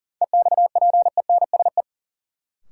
Morse code, heard as a keyed tone, about 40 wpm, 700 Hz. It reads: EXPENSE
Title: Sets of 4 Words - New Word Drill - 40wpm